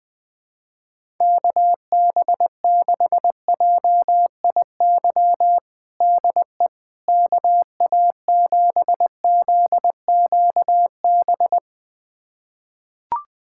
Morse code, heard as a keyed tone, about 20 wpm, 700 Hz. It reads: KB6JIY DE KA7ZQB
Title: USA Callsign DE Callsign - 6 Characters - 20wpm